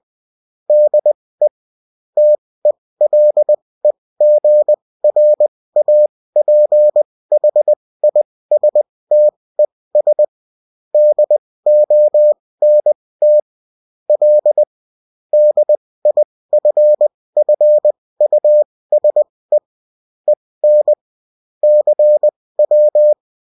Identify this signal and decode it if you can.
Morse code, heard as a keyed tone, about 20 wpm, spaced as for 16 wpm, 600 Hz. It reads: DE TELEGRAPHISTES DONT L DIFFUSE EN CW
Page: bulletin-uft-sous-forme-texte-et-en-cw